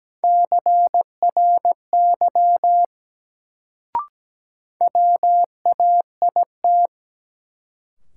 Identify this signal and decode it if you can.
Morse code, heard as a keyed tone, about 17 wpm, 700 Hz. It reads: CRY WAIT
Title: Sets of 5 Words Encore - New Word Drill - 17wpm